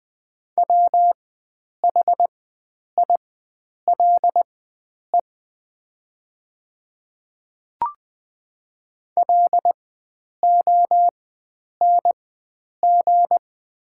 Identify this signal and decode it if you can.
Morse code, heard as a keyed tone, about 20 wpm, spaced as for 9 wpm, 700 Hz. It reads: WHILE LONG
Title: 4x Character Spacing - Top 300 Words - 20wpm